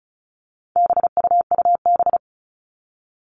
Morse code, heard as a keyed tone, about 34 wpm, 700 Hz. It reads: BUUB
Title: All Cut Numbers - 4-Digit 35wpm